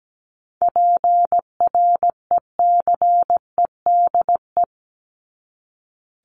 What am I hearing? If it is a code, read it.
Morse code, heard as a keyed tone, about 17 wpm, 700 Hz. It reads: PRECEDE
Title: ING Suffix - Top 100 Words with Base Words 17wpm